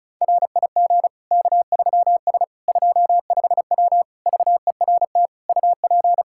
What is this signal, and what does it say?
Morse code, heard as a keyed tone, about 34 wpm, 700 Hz. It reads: RIG K3S 25W VERT UP
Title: K1USN SST QSO Practice 35wpm